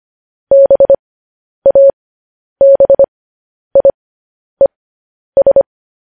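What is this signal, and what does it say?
Morse code, heard as a keyed tone, about 25 wpm, spaced as for 10 wpm, 550 Hz. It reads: BABIES
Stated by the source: Morse code news headlines at 10.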